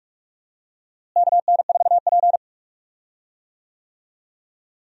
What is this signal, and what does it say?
Morse code, heard as a keyed tone, about 46 wpm, 700 Hz. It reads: KN4P
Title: USA Call Signs - 4 Characters - Repeated 3x - 45wpm